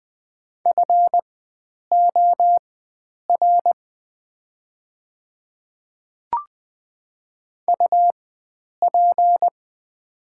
Morse code, heard as a keyed tone, about 20 wpm, spaced as for 9 wpm, 690 Hz. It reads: FOR UP
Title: 4x Character Spacing - Top 100 Words - 20wpm